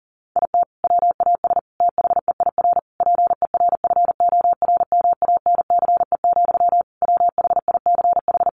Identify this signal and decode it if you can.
Morse code, heard as a keyed tone, about 40 wpm, 700 Hz. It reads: IT WAS THEIR PERFORMANCE, WHICH
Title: Sentences from Top 1500 Words 40wpm